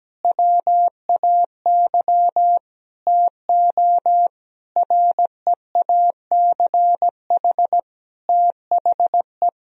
Morse code, heard as a keyed tone, about 17 wpm, 700 Hz. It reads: WAY TO REACH THE
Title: Sentences from Top 1000 Words - Review Part 2 - 17wpm